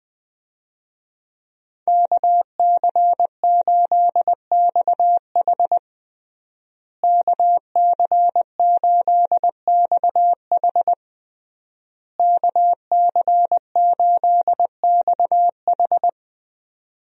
Morse code, heard as a keyed tone, about 20 wpm, 700 Hz. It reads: KC8XH KC8XH KC8XH
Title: USA Call Signs - 5 Characters - Repeated 3x - 20wpm